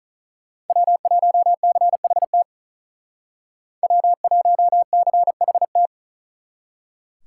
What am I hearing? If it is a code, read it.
Morse code, heard as a keyed tone, about 46 wpm, 700 Hz. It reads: W1CHT W1CHT